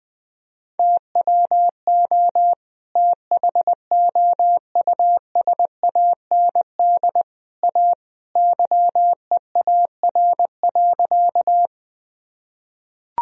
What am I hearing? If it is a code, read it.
Morse code, heard as a keyed tone, about 21 wpm, 700 Hz. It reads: TWO THOUSAND A YEAR.
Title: Sentences from Top 1000 Words - Review Part 3 - 20wpm